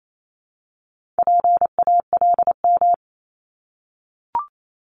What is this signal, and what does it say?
Morse code, heard as a keyed tone, about 28 wpm, 700 Hz. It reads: PALM
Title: Multi-letter Phonemes Repeated 3x 28wpm